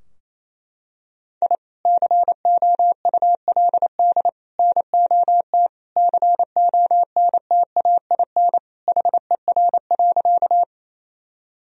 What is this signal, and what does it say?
Morse code, heard as a keyed tone, about 28 wpm, 700 Hz. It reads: I COULD NOT CONTAIN HER.
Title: Sentences from Top 1000 Words - Review Part 2 - 28wpm